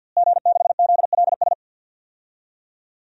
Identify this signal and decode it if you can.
Morse code, heard as a keyed tone, about 52 wpm, 700 Hz. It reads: G6ZLS
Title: DXCC-100 Call Signs - 5 Characters - Repeated 3x - 50wpm